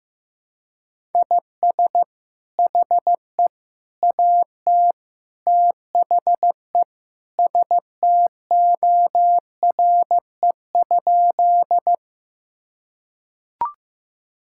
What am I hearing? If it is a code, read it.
Morse code, heard as a keyed tone, about 15 wpm, 700 Hz. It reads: IS HE AT THE STORE?
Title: Sentences from Top 1000 Words - Review Part 5 - 15wpm